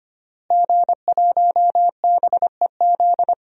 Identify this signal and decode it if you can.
Morse code, heard as a keyed tone, about 25 wpm, 700 Hz. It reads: G1BEZ
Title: DXCC-100 Call Signs - 5 Characters - Repeated 2x - 25wpm